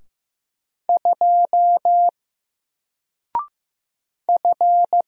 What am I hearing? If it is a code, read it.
Morse code, heard as a keyed tone, about 15 wpm, 700 Hz. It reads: 2 F
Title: Single Letter-Number Alt 15wpm